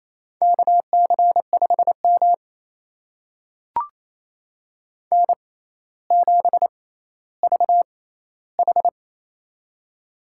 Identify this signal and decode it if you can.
Morse code, heard as a keyed tone, about 28 wpm, spaced as for 14 wpm, 700 Hz. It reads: KC5M N7VH
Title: USA Call Signs - 4 Characters - 6x Character Spacing - 28wpm